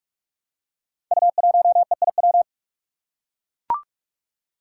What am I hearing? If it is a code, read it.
Morse code, heard as a keyed tone, about 44 wpm, 700 Hz. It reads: U1EIW
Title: Five Challenging Letter-Number Combinations 45wpm